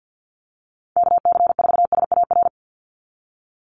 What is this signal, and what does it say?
Morse code, heard as a keyed tone, about 50 wpm, 700 Hz. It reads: KC4SAR
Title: USA Call Signs 50wpm